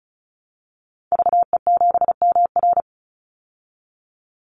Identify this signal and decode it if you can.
Morse code, heard as a keyed tone, about 34 wpm, 700 Hz. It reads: VE7MR
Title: Call Signs Repeated 2x - 35wpm